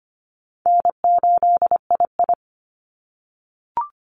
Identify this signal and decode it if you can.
Morse code, heard as a keyed tone, about 25 wpm, 700 Hz. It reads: N8II
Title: Call Signs - 25wpm